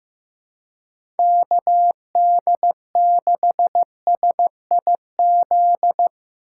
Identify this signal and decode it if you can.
Morse code, heard as a keyed tone, about 15 wpm, 700 Hz. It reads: KD6SIZ
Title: USA Call Signs - 6 Characters - 3x Character Spacing - 15wpm